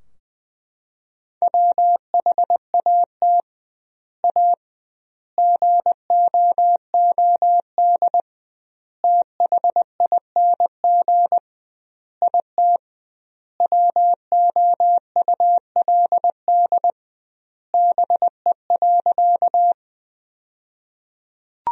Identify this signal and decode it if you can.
Morse code, heard as a keyed tone, about 21 wpm, 700 Hz. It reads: WHAT A GOOD THING IT WOULD BE.
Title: 2x Word-Spacing - Sentences from Top 200 Words - 20wpm